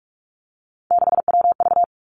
Morse code, heard as a keyed tone, about 46 wpm, 700 Hz. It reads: 6WV
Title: Three Challenging Letter-Number Combinations 45wpm